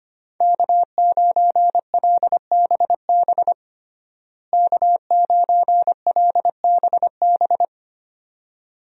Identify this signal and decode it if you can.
Morse code, heard as a keyed tone, about 25 wpm, 700 Hz. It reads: K9LBB K9LBB
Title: USA Call Signs - 5 Characters - Repeated 2x - 25wpm